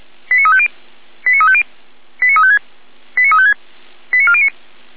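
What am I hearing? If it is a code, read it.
Five-tone selective call (ZVEI), heard as quick runs of five tones, tones of about 70 ms. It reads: 89350 89350 89356 89356 89409
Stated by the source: ZVEI five-tone call 89104 received by SDR, fire brigade Orlamünde (feuerwehr-orlamuende).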